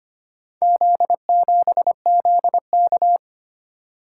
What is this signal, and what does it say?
Morse code, heard as a keyed tone, about 25 wpm, 700 Hz. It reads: Z7ZK
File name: practice / 25wpm / Four Letter-Number Combinations - 25wpm